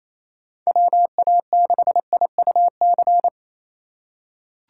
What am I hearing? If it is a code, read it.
Morse code, heard as a keyed tone, about 28 wpm, 700 Hz. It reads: WA6IUC
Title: USA Call Signs - 6 Characters - 3x Character Spacing - 28wpm